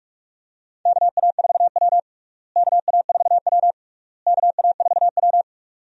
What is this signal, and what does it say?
Morse code, heard as a keyed tone, about 46 wpm, 700 Hz. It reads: KA4W KA4W KA4W
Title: USA Call Signs - 4 Characters - Repeated 3x - 45wpm